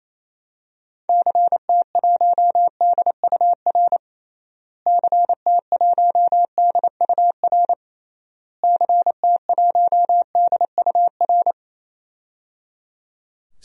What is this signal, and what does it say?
Morse code, heard as a keyed tone, about 28 wpm, 700 Hz. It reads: CT1DUR CT1DUR CT1DUR
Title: DXCC-100 Call Signs - 6 Characters - Repeated 3x - 28wpm